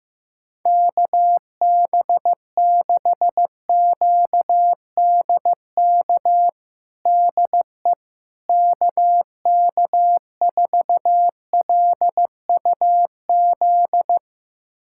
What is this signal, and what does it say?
Morse code, heard as a keyed tone, about 15 wpm, 700 Hz. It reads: KB6QDK DE KK4LUZ
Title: USA Callsign DE Callsign 15wpm